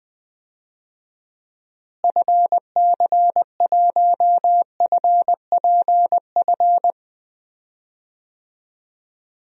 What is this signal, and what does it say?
Morse code, heard as a keyed tone, about 21 wpm, 700 Hz. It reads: FC1FPF
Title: DXCC-100 Call Signs - 6 Characters - Repeated 3x - 20wpm